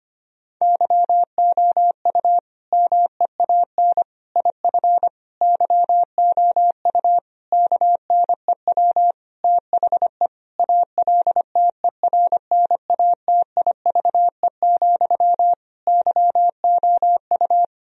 Morse code, heard as a keyed tone, about 25 wpm, 700 Hz. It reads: YOU MEAN IF YOU KNEW THE ALTERNATIVE, YOU
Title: Sentences from Top 1000 Words - Review Part 4 - 25wpm